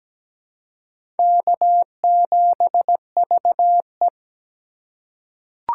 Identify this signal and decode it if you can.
Morse code, heard as a keyed tone, about 17 wpm, 700 Hz. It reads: K7VE
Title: USA Call Signs - 4 Characters - 3x Character Spacing - 17wpm